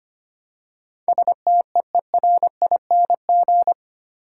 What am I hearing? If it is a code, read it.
Morse code, heard as a keyed tone, about 25 wpm, 690 Hz. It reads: STEERING
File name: practice / 25wpm / Easy Words with Hints - 25wpm